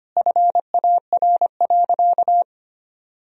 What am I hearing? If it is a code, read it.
Morse code, heard as a keyed tone, about 25 wpm, 700 Hz. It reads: FAR.
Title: Sentences from Top 1000 Words - Review Part 1 - 25wpm